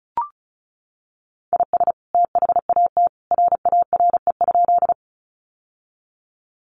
Morse code, heard as a keyed tone, about 34 wpm, 700 Hz. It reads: IS THAT RARE?
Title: Sentences from Top 1100 Words 35wpm